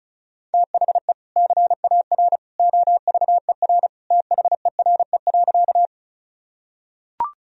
Morse code, heard as a keyed tone, about 34 wpm, 700 Hz. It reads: THE CAR OVER THERE.
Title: Sentences from Top 1000 Words - Review Part 1 - 35wpm